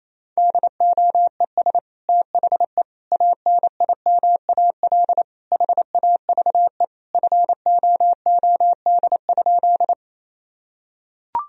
Morse code, heard as a keyed tone, about 28 wpm, 700 Hz. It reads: DOES THE ANIMAL HAVE FOOD?
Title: Sentences from Top 1000 Words - Review Part 1 - 28wpm